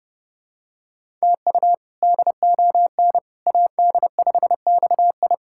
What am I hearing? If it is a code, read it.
Morse code, heard as a keyed tone, about 30 wpm, 700 Hz. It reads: TU DON AD5XI
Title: K1USN SST QSO Practice - 30wpm